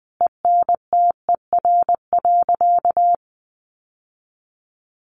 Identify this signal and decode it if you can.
Morse code, heard as a keyed tone, about 20 wpm, 700 Hz. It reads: ENTER.
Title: Sentences from Top 1300 Words 20wpm